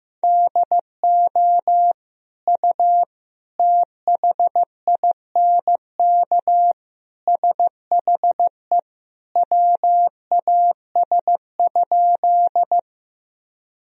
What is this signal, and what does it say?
Morse code, heard as a keyed tone, about 15 wpm, 700 Hz. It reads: DO U THINK SHE WAS?
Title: Sentences from Top 200 Words and Abbreviations 15wpm